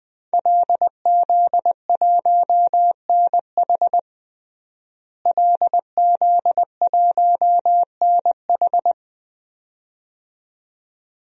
Morse code, heard as a keyed tone, about 20 wpm, 700 Hz. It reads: LZ1NH LZ1NH
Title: DXCC-100 Call Signs - 5 Characters - Repeated 2x - 20wpm